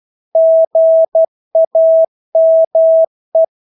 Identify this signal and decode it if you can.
Morse code, heard as a keyed tone, about 12 wpm, 650 Hz. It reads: GAME
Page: Engelske ord 12 wpm